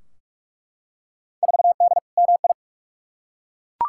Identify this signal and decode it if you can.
Morse code, heard as a keyed tone, about 46 wpm, 700 Hz. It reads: 4D MI
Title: Field Day Exchanges Encore - 45wpm